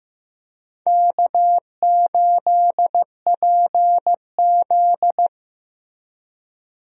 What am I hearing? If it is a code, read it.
Morse code, heard as a keyed tone, about 15 wpm, 700 Hz. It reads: K8PZ
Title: USA Call Signs - 4 Characters - Repeated 3x - 15wpm